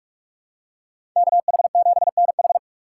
Morse code, heard as a keyed tone, about 43 wpm, 700 Hz. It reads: KH7NH
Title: USA Call Signs - 6x Character Spacing - 45wpm